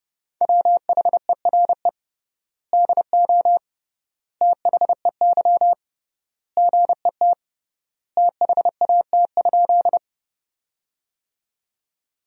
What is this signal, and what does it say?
Morse code, heard as a keyed tone, about 32 wpm, 700 Hz. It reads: WHERE DO THEY GET THAT?
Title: 3x Word-Spacing - Sentences from Top 200 Words - 30wpm